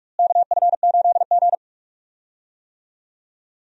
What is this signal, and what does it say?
Morse code, heard as a keyed tone, about 43 wpm, 700 Hz. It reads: KF8G
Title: USA Call Signs - 4 Characters - Repeated 2x - 45wpm